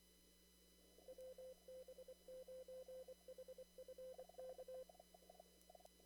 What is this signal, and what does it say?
Morse code, heard as a keyed tone, about 27 wpm, 520 Hz. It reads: WB9HFK
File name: CW / kt5j wpx cw-op-k5pi